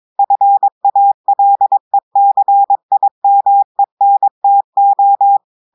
Morse code, heard as a keyed tone, about 22 wpm, 800 Hz. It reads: FALECIMENTO
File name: audios / portugais / portugais texte en clair 1 vitesse 22 mots minutes